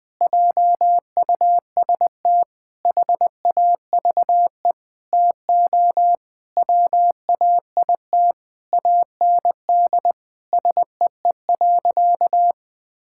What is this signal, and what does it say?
Morse code, heard as a keyed tone, about 20 wpm, 700 Hz. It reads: JUST HAVE TO WAIT AND SEE.
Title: Sentences with 7-Word Phrases - Spoken Beginning - 20wpm